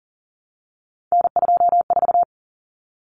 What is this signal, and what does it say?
Morse code, heard as a keyed tone, about 40 wpm, 700 Hz. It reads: N24
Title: Common Cut Numbers - 3-Digit 40wpm